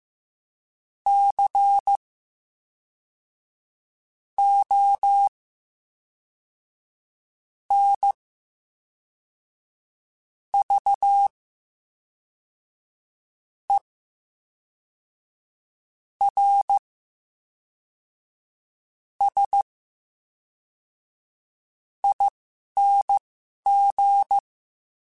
Morse code, heard as a keyed tone, about 15 wpm, spaced as for 4 wpm, 800 Hz. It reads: CONVERSING
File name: Special INGs / Special INGs-mp3s / VerySlowToFaster Ing @10